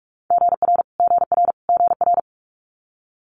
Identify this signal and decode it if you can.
Morse code, heard as a keyed tone, about 46 wpm, 690 Hz. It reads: GR GR GR
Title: Multi-letter Phonemes Repeated 3x 45wpm